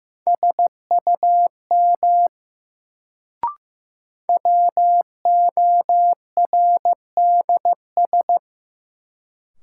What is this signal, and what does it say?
Morse code, heard as a keyed tone, about 15 wpm, 700 Hz. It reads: SUM WORDS
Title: Sentences from Top 1000 Words - Review Part 6 - New Word Drill - 15wpm